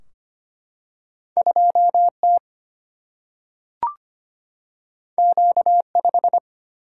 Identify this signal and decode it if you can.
Morse code, heard as a keyed tone, about 25 wpm, 700 Hz. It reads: 2 T Q 5
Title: Single-Letter Single-Number 25wpm